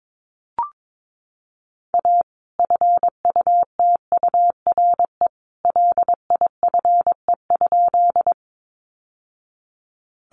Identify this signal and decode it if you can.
Morse code, heard as a keyed tone, about 22 wpm, 700 Hz. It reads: A FUTURE LIFE?
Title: Sentences from Top 700 Words 22wpm